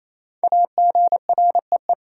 Morse code, heard as a keyed tone, about 28 wpm, 700 Hz. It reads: AGREE
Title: Sets of 5 Words Encore - New Word Drill - 28wpm